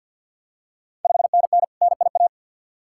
Morse code, heard as a keyed tone, about 51 wpm, 690 Hz. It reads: 5NN NIA
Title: Ontario QSO Party Exchange 50wpm